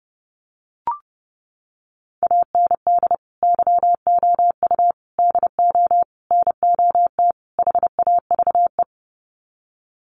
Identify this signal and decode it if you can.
Morse code, heard as a keyed tone, about 30 wpm, 700 Hz. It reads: AND YOU DO NOT HAVE
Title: Sets of 5 Words Encore 30wpm